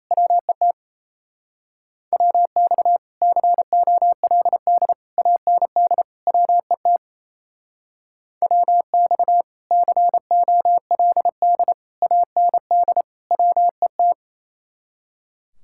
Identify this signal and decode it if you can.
Morse code, heard as a keyed tone, about 33 wpm, 700 Hz. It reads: WET WX COLD AND WET WX COLD AND WET
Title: Speed-Racing Common QSO Phrases - 28wpm